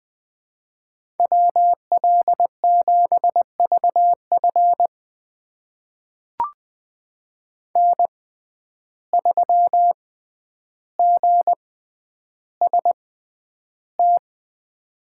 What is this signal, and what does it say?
Morse code, heard as a keyed tone, about 20 wpm, spaced as for 10 wpm, 700 Hz. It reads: WL7VF N3GST
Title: USA Call Signs - 5 Characters - 6x Character Spacing - 20wpm